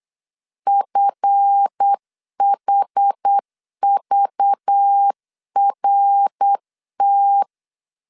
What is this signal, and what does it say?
Morse code, heard as a keyed tone, about 8 wpm, 800 Hz. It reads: FHVRT
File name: LEZIONI / 7 GF9Z / 7 GF 8wpm 3108